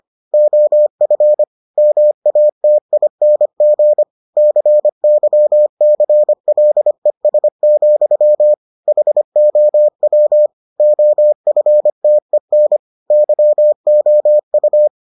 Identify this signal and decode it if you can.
Morse code, heard as a keyed tone, about 25 wpm, 600 Hz. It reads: OF MATING CYCLES, HOW OFTEN YOU